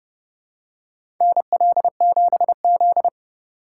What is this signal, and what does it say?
Morse code, heard as a keyed tone, about 30 wpm, 700 Hz. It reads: NL7Z
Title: USA Call Signs - 2x Character Spacing - 30wpm